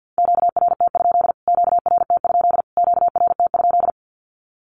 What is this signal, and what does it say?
Morse code, heard as a keyed tone, about 50 wpm, 700 Hz. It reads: QRT? QRT? QRT?
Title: Q-Codes Repeated 3x 50wpm